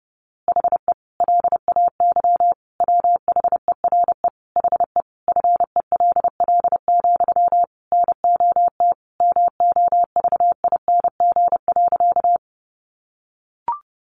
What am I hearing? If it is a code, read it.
Morse code, heard as a keyed tone, about 30 wpm, 700 Hz. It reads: HE LAY WHERE HE FELL, NOT MOVING.
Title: A Sound Of Thunder 30wpm